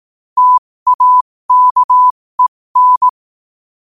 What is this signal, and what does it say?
Morse code, heard as a keyed tone, about 18 wpm, spaced as for 15 wpm, 1000 Hz. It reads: TAKEN
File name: qotdfiles / 15wpm